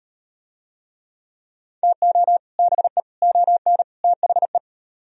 Morse code, heard as a keyed tone, about 36 wpm, 700 Hz. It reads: TO BE ON THE